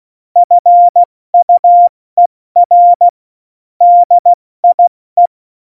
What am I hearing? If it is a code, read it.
Morse code, heard as a keyed tone, about 16 wpm, 700 Hz. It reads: FUER DIE
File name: Morsekurs / Witze / Witz 07 Juli 2020, 70 F80